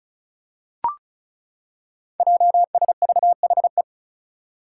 Morse code, heard as a keyed tone, about 36 wpm, 700 Hz. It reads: JSVHE
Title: Five Challenging Letter-Number Combinations 35wpm